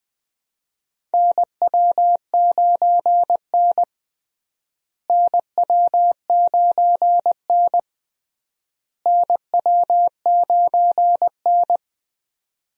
Morse code, heard as a keyed tone, about 20 wpm, 700 Hz. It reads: NW9N NW9N NW9N
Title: USA Call Signs - 4 Characters - Repeated 3x - 20wpm